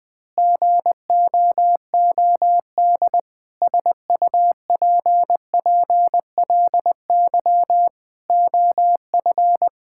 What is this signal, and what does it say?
Morse code, heard as a keyed tone, about 21 wpm, 700 Hz. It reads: GOOD SUPPLY OF
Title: Sentences from Top 1000 Words - Review Part 5 - 20wpm